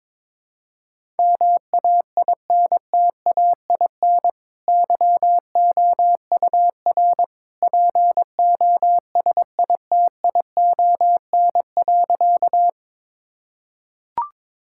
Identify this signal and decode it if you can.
Morse code, heard as a keyed tone, about 22 wpm, 700 Hz. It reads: MAINTAIN YOUR POSITION.
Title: Sentences from Top 1000 Words - Review Part 5 - 22wpm